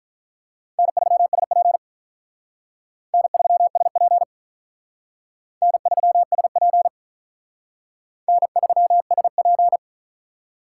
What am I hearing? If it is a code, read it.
Morse code, heard as a keyed tone, about 43 wpm, 700 Hz. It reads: N3SP N3SP N3SP N3SP
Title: Speed Racing - USA Call Signs 35wpm